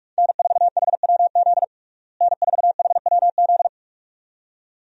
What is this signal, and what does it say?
Morse code, heard as a keyed tone, about 42 wpm, 700 Hz. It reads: N4HWZ N4HWZ
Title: USA Call Signs - 5 Characters - Repeated 2x - 45wpm